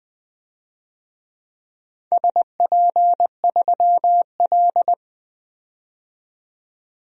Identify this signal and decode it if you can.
Morse code, heard as a keyed tone, about 21 wpm, 700 Hz. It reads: SP3L
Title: DXCC-100 Call Signs - 4 Characters - 20wpm